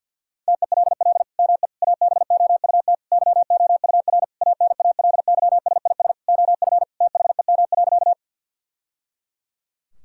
Morse code, heard as a keyed tone, about 50 wpm, 690 Hz. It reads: TELL ME ABOUT YOUR ANALYSIS OF THEM.
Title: Sentences from Top 1000 Words - Review Part 5 - 50wpm